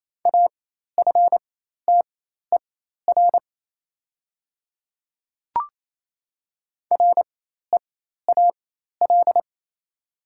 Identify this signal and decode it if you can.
Morse code, heard as a keyed tone, about 27 wpm, spaced as for 13 wpm, 690 Hz. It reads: AFTER REAL
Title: 4x Character Spacing - Top 300 Words - 28wpm